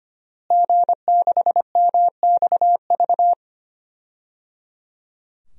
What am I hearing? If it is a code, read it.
Morse code, heard as a keyed tone, about 25 wpm, 700 Hz. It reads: G6MXV
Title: DXCC-100 Call Signs - 5 Characters - Repeated 3x - 25wpm